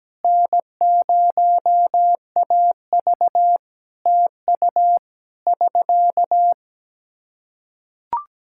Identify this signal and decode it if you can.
Morse code, heard as a keyed tone, about 17 wpm, 700 Hz. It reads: N0AV TU <SK>
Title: K1USN SST QSO Practice 17wpm